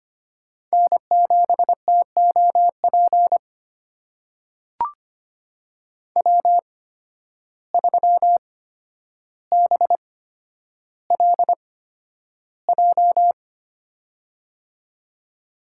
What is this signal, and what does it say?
Morse code, heard as a keyed tone, about 25 wpm, spaced as for 11 wpm, 700 Hz. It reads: N7TOP W3BLJ
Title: USA Call Signs - 5 Characters - 8x Character Spacing - 25wpm